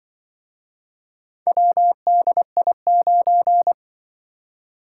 Morse code, heard as a keyed tone, about 24 wpm, 700 Hz. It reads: WDI9
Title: Speed-Racing - Four Letter-Number Combinations - 20wpm